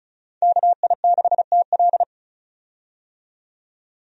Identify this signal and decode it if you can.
Morse code, heard as a keyed tone, about 36 wpm, 700 Hz. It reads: KI6TL
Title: USA Call Signs - 5 Characters - 35wpm